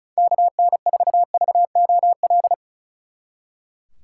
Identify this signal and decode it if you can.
Morse code, heard as a keyed tone, about 46 wpm, 700 Hz. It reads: KN4VOL